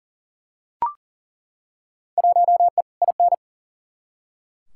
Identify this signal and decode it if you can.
Morse code, heard as a keyed tone, about 39 wpm, 700 Hz. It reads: 1E IN
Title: Field Day Exchanges Encore - 40wpm